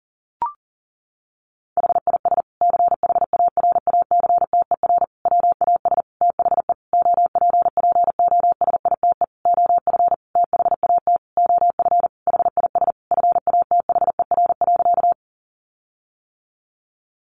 Morse code, heard as a keyed tone, about 37 wpm, 700 Hz. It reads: HIS CHARACTER WAS THE OPPOSITE OF THAT OF HIS FATHER.
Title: Sentences from Top 1300 Words 40wpm